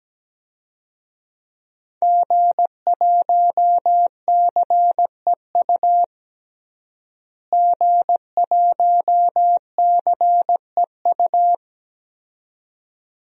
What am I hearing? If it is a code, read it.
Morse code, heard as a keyed tone, about 17 wpm, 700 Hz. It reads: G1CEU G1CEU
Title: DXCC-100 Call Signs - 5 Characters - Repeated 2x - 17wpm